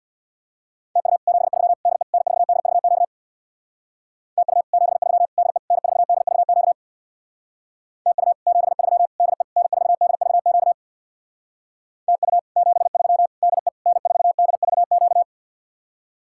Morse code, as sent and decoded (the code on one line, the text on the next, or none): - ..- / --... ...-- / -.. / -. ....- -.. ...- --.- / - ..- / --... ...-- / -.. / -. ....- -.. ...- --.- / - ..- / --... ...-- / -.. / -. ....- -.. ...- --.- / - ..- / --... ...-- / -.. / -. ....- -.. ...- --.-
TU 73 DE N4DVQ TU 73 DE N4DVQ TU 73 DE N4DVQ TU 73 DE N4DVQ